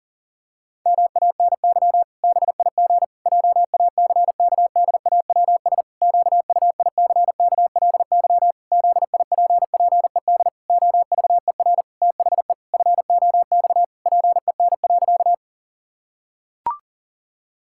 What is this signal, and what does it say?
Morse code, heard as a keyed tone, about 40 wpm, 700 Hz. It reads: MANY BIG JACKDAWS QUICKLY ZIPPED OVER THE FOX PEN.
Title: Pangram Challenge v2 - 40wpm